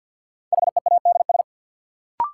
Morse code, heard as a keyed tone, about 53 wpm, 700 Hz. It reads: HEADS
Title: Sentences from Top 1000 Words - Review Part 3 - New Word Drill - 50wpm